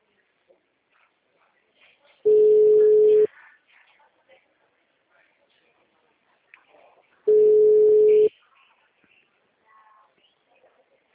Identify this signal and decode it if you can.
Ringback tone (Europe, 425 Hz), with speech.
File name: out- -2010- - - .87561